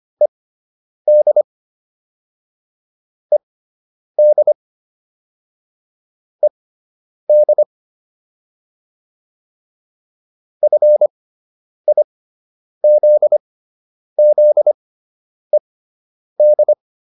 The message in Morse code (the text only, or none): ED ED ED FIZZED